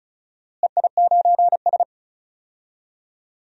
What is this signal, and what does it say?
Morse code, heard as a keyed tone, about 34 wpm, 700 Hz. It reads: EI9S
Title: DXCC-100 Call Signs - 4 Characters - Repeated 3x - 35wpm